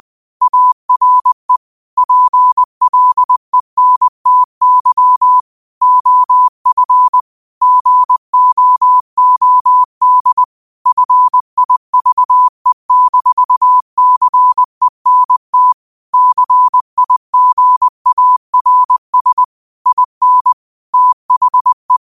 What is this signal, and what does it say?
Morse code, heard as a keyed tone, about 20 wpm, 1000 Hz. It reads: ARE PLENTY OF GOOD FIVE-CENT CIGARS IN THE
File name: qotdfiles / 20wpm